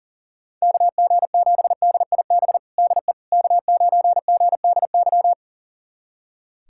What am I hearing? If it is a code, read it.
Morse code, heard as a keyed tone, about 40 wpm, 700 Hz. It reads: KG7DIB DE K9GDY
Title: USA Callsign DE Callsign 40wpm